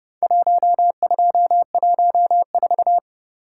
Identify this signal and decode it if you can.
Morse code, heard as a keyed tone, about 30 wpm, 700 Hz. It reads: 1214